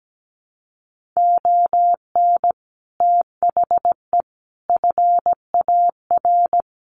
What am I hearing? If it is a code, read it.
Morse code, heard as a keyed tone, about 17 wpm, 700 Hz. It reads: ON THE FAR
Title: Sets of 5 Words 17wpm